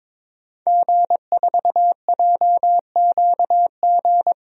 Morse code, heard as a keyed tone, about 22 wpm, 700 Hz. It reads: G4JQG
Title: DXCC-100 Call Signs - 5 Characters - 22wpm